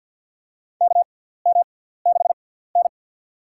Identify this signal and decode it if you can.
Morse code, heard as a keyed tone, about 53 wpm, spaced as for 17 wpm, 700 Hz. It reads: KM6N
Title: USA Call Signs - 4 Characters - 6x Character Spacing - 50wpm